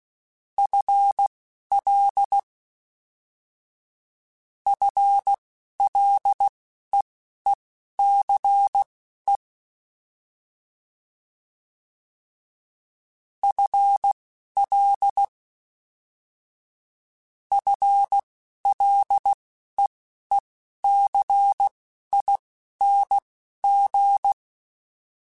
Morse code, heard as a keyed tone, about 16 wpm, spaced as for 11 wpm, 800 Hz. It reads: FL FLEECE FL FLEECING
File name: FL / FL mp3s / FL Prefixes 1x1 @10